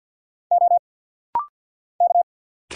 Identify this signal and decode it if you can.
Morse code, heard as a keyed tone, about 50 wpm, 700 Hz. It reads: O K
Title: Single Letter-Number Alt - Rapid-Fire 50wpm